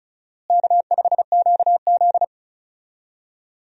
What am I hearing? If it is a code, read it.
Morse code, heard as a keyed tone, about 36 wpm, 700 Hz. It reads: K5QZ
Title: USA Call Signs - 4 Characters - Repeated 2x - 35wpm